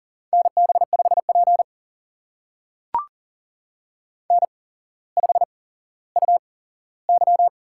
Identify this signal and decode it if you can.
Morse code, heard as a keyed tone, about 39 wpm, spaced as for 17 wpm, 700 Hz. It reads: NB5P N5UY
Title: USA Call Signs - 4 Characters - 8x Character Spacing - 40wpm